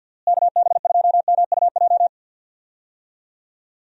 Morse code, heard as a keyed tone, about 52 wpm, 700 Hz. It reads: KB2MUJ
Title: USA Call Signs - 6 Characters - Repeated 3x - 50wpm